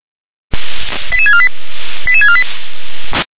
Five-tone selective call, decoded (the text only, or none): 88648 88648